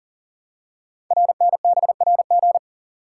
Morse code, heard as a keyed tone, about 38 wpm, 700 Hz. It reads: RNBRG
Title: Code Groups - 5 Letters - 40wpm